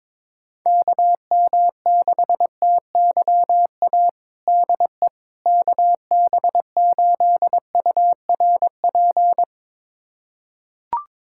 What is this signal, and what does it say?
Morse code, heard as a keyed tone, about 22 wpm, 700 Hz. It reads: KM6TYA DE KB8URP
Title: USA Callsign DE Callsign 22wpm